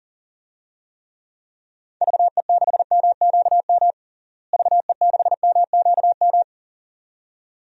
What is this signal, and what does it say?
Morse code, heard as a keyed tone, about 40 wpm, 700 Hz. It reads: VE6MQM VE6MQM
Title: DXCC-100 Call Signs - 6 Characters - Repeated 2x - 40wpm